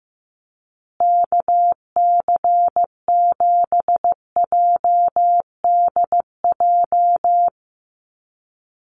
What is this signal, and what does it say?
Morse code, heard as a keyed tone, about 15 wpm, 700 Hz. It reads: KC7JDJ
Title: USA Call Signs Repeated 2x 15wpm